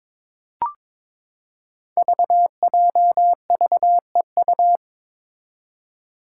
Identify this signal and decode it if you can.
Morse code, heard as a keyed tone, about 21 wpm, 700 Hz. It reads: VJVEU
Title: Five Challenging Letter-Number Combinations 22wpm